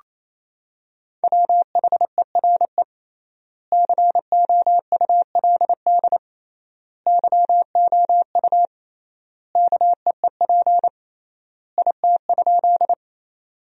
Morse code, heard as a keyed tone, about 28 wpm, 700 Hz. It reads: WHERE COULD YOU KEEP IT?
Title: 3x Word-Spacing - Sentences from Top 300 Words - 28wpm